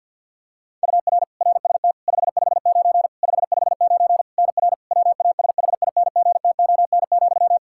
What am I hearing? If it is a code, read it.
Morse code, heard as a keyed tone, about 50 wpm, 700 Hz. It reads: UR RST 559 559 NR WASHINGTON,
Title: Practice QSO - Exchange 1 - Beginning - 50wpm